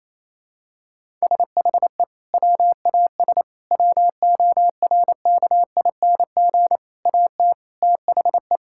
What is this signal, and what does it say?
Morse code, heard as a keyed tone, about 28 wpm, 700 Hz. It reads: SHE WAS WORKING AT THE
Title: Sentences from Top 1000 Words - Review Part 5 - 28wpm